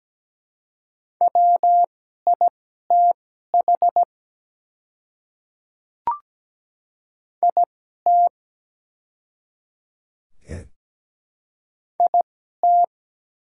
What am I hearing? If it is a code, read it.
Morse code, heard as a keyed tone, about 17 wpm, spaced as for 12 wpm, 700 Hz. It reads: WITH IT IT
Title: 2x Character Spacing - Top 200 Words - 17wpm